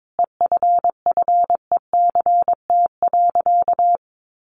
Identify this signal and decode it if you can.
Morse code, heard as a keyed tone, about 22 wpm, 700 Hz. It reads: EFFECT.
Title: Sentences from Top 1400 Words 22wpm